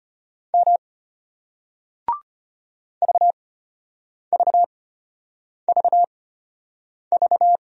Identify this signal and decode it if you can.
Morse code, heard as a keyed tone, about 34 wpm, spaced as for 8 wpm, 700 Hz. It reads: M VVVV